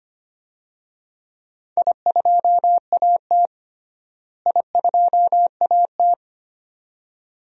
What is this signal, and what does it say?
Morse code, heard as a keyed tone, about 25 wpm, 700 Hz. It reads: I2AT I2AT
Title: DXCC-100 Call Signs - 4 Characters - Repeated 2x - 25wpm